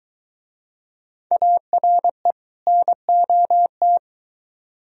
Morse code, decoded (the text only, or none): ARE NOT